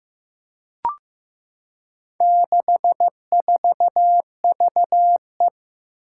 Morse code, heard as a keyed tone, about 15 wpm, 700 Hz. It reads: 64VE
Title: Four Challenging Letter-Number Combinations 15wpm